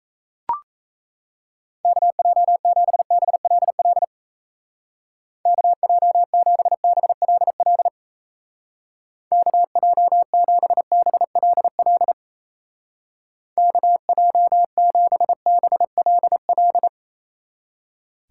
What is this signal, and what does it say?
Morse code, heard as a keyed tone, about 35 wpm, 700 Hz. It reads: KJ7BLL KJ7BLL KJ7BLL KJ7BLL
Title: Speed Racing - USA 2x3 Call Signs 28wpm